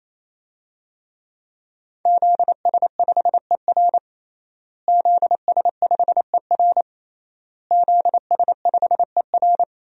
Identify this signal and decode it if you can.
Morse code, heard as a keyed tone, about 28 wpm, 700 Hz. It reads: ZS5ER ZS5ER ZS5ER
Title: DXCC-100 Call Signs - 5 Characters - Repeated 3x - 28wpm